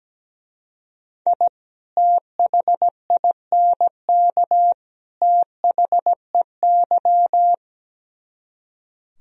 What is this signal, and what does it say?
Morse code, heard as a keyed tone, about 17 wpm, 700 Hz. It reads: I THINK THEY
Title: Speed-Racing - Sets of 3 Words 17wpm